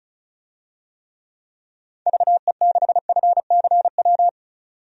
Morse code, heard as a keyed tone, about 36 wpm, 700 Hz. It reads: VE6FCW
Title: DXCC-100 Call Signs - 6 Characters - 35wpm